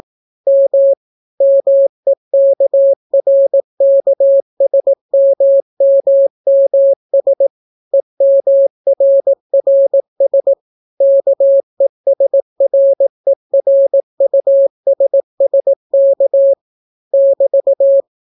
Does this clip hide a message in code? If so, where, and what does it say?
Yes: Morse code from 0.5 to 18.0 s, heard as a keyed tone, about 18 wpm, 550 Hz. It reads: M MEKRKSMMMS EMRRS KESRERUSSK <BT>